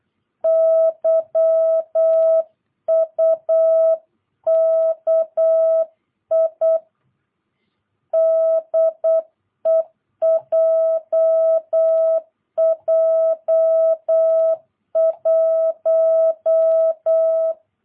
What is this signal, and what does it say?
Morse code, heard as a keyed tone, about 8 wpm, 650 Hz. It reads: YUKI DE JJ1
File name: jh8cbh / 2022cw